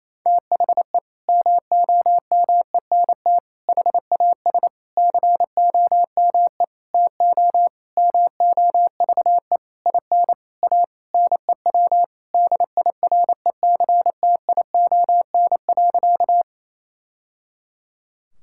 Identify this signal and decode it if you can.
Morse code, heard as a keyed tone, about 28 wpm, 700 Hz. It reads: THE MOMENT HAS COME TO MOVE IN A NEW DIRECTION.
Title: Sentences from Top 1000 Words - Review Part 5 - 28wpm